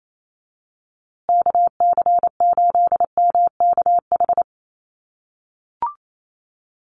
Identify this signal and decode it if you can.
Morse code, heard as a keyed tone, about 28 wpm, 700 Hz. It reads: KC8MKH
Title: USA Call Signs Repeated 2x 28wpm